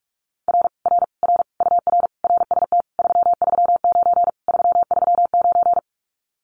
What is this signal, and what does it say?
Morse code, heard as a keyed tone, about 46 wpm, 700 Hz. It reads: R R R UR RST 339 339
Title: Common QSO Phrases Encore 45wpm